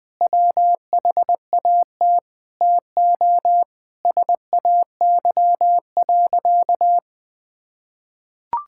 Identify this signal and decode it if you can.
Morse code, heard as a keyed tone, about 20 wpm, 700 Hz. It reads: WHAT TO SAY.
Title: Sentences with 5-Word Phrases - 20wpm